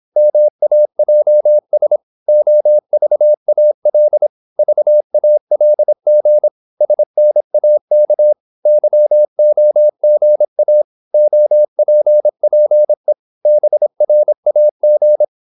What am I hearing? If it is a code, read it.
Morse code, heard as a keyed tone, about 26 wpm, 600 Hz. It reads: MAJS OVAL VALG SNAK YOGA OPPE BRAG
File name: Korte ord DK 26wpm